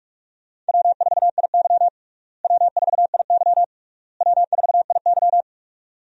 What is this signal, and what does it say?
Morse code, heard as a keyed tone, about 45 wpm, 700 Hz. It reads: W4IY W4IY W4IY
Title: USA Call Signs - 4 Characters - Repeated 3x - 45wpm